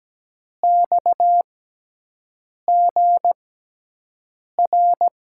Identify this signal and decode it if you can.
Morse code, heard as a keyed tone, about 17 wpm, spaced as for 6 wpm, 700 Hz. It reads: XGR
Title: USA Call Signs - 6x Character Spacing - 17wpm